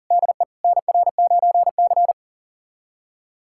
Morse code, heard as a keyed tone, about 37 wpm, 700 Hz. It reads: DE NR9C
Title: USA Callsign DE Callsign - 4 Characters - 40wpm